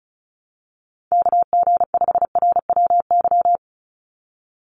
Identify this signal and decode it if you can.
Morse code, heard as a keyed tone, about 36 wpm, 700 Hz. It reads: KG5RWY
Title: USA Call Signs 35wpm